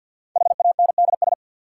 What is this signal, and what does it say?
Morse code, heard as a keyed tone, about 53 wpm, 700 Hz. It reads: HANDS
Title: Sets of 4 Words - New Word Drill - 50wpm